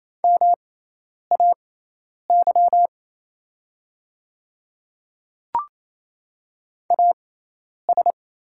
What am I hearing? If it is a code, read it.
Morse code, heard as a keyed tone, about 27 wpm, spaced as for 10 wpm, 700 Hz. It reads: MAY AS
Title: 6x Character Spacing - Top 100 Words - 28wpm